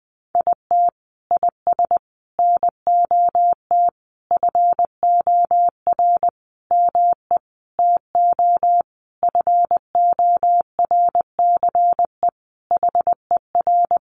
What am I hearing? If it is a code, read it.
Morse code, heard as a keyed tone, about 19 wpm, 700 Hz. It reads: IT IS NOT FOR ME TO FORCE HER
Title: Sentences from Top 1500 Words 20wpm